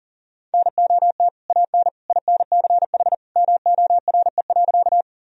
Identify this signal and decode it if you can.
Morse code, heard as a keyed tone, about 40 wpm, 700 Hz. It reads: NOT AN INCH MORE.
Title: Sentences from Top 1000 Words - Review Part 3 - 40wpm